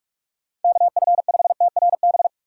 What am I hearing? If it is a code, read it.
Morse code, heard as a keyed tone, about 46 wpm, 700 Hz. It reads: KF5TRB
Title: USA Call Signs - 2x Character Spacing - 45wpm